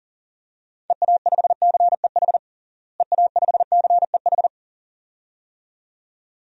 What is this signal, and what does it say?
Morse code, heard as a keyed tone, about 40 wpm, 700 Hz. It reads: EA5CEH EA5CEH
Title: DXCC-100 Call Signs - 6 Characters - Repeated 2x - 40wpm